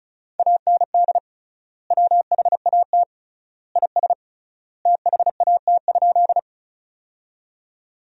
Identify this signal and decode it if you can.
Morse code, heard as a keyed tone, about 34 wpm, 700 Hz. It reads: AND WHAT IS THAT?
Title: 3x Word-Spacing - Sentences from Top 100 Words - 35wpm